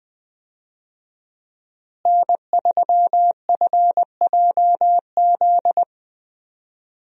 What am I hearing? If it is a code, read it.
Morse code, heard as a keyed tone, about 20 wpm, 700 Hz. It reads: N3FJZ
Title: USA Call Signs - 5 Characters - Repeated 2x - 20wpm